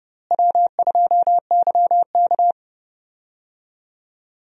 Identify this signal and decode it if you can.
Morse code, heard as a keyed tone, about 30 wpm, 700 Hz. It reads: W2YK
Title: USA Call Signs - 4 Characters - Repeated 2x - 30wpm